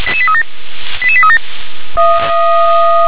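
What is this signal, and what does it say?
Five-tone selective call (ZVEI), heard as quick runs of five tones, tones of about 70 ms. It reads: 88837
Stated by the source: Five-tone ZVEI selective call received by SDR.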